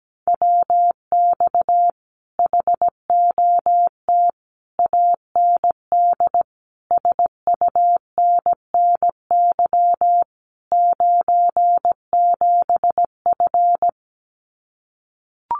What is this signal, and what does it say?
Morse code, heard as a keyed tone, about 17 wpm, 700 Hz. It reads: WX HOT AND SUNNY 97F
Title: Common QSO Phrases Encore 17wpm